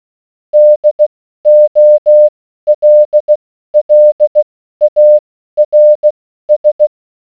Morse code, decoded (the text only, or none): DOLLARS